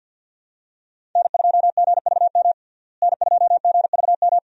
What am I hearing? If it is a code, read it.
Morse code, heard as a keyed tone, about 47 wpm, 700 Hz. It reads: N2GVM N2GVM
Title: USA Call Signs - 5 Characters - Repeated 2x - 50wpm